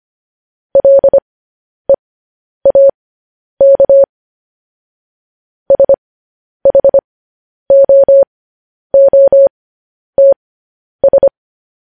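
Morse code, heard as a keyed tone, about 25 wpm, spaced as for 10 wpm, 550 Hz. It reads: LEAK SHOOTS
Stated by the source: Morse code news headlines sent at 10 wpm.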